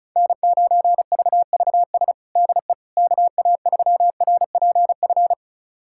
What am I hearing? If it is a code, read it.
Morse code, heard as a keyed tone, about 36 wpm, 700 Hz. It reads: N9VVS DE KA3RPF
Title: USA Callsign DE Callsign 35wpm